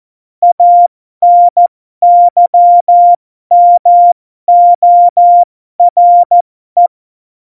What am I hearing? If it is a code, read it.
Morse code, heard as a keyed tone, about 14 wpm, spaced as for 12 wpm, 700 Hz. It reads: ANYMORE
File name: Morsekurs / Witze